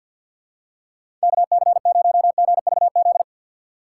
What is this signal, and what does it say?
Morse code, heard as a keyed tone, about 51 wpm, 700 Hz. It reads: KC0GVZ
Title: USA Call Signs - 2x Character Spacing - 50wpm